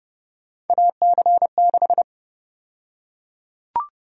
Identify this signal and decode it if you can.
Morse code, heard as a keyed tone, about 30 wpm, 700 Hz. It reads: AC6
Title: USA Call Sign Prefixes - 30wpm